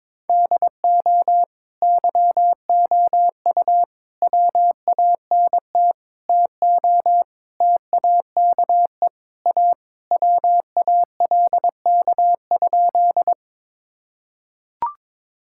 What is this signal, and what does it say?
Morse code, heard as a keyed tone, about 22 wpm, 700 Hz. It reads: DO YOU WANT TO TAKE A WALK?
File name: practice / 22wpm / Sentences with 3-Word Phrases - 22wpm